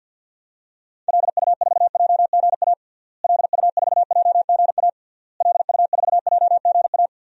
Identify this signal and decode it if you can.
Morse code, heard as a keyed tone, about 52 wpm, 700 Hz. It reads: LU4JGA LU4JGA LU4JGA
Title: DXCC-100 Call Signs - 6 Characters - Repeated 3x - 50wpm